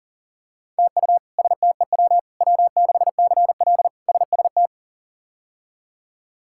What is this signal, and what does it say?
Morse code, heard as a keyed tone, about 40 wpm, 700 Hz. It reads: TU STEW W6CL SST
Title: K1USN SST QSO Practice - 40wpm